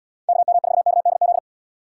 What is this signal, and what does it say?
Morse code, heard as a keyed tone, about 77 wpm, 700 Hz. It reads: KM4KDZ